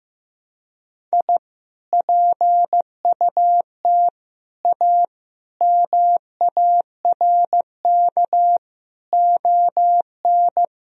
Morse code, heard as a keyed tone, about 15 wpm, 700 Hz. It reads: I PUT A MARK ON
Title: Sentences from Top 1000 Words - Review Part 1 - 15wpm